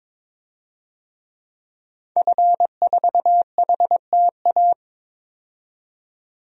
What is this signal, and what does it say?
Morse code, heard as a keyed tone, about 22 wpm, 700 Hz. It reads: F4HTA
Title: DXCC-100 Call Signs - 5 Characters - 22wpm